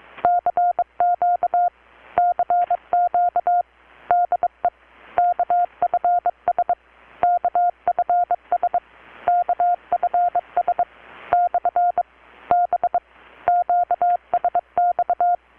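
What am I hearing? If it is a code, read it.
Morse code, heard as a keyed tone, about 23 wpm, 720 Hz. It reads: CQ CQ DE KFS KFS KFS / B QSX
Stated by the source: Morse from KPH's Night of Nights XX maritime radiotelegraphy event, received in Chiapas, Mexico about 3,800 km away on a Kenwood R-600 with a grounded 10 m long-wire antenna.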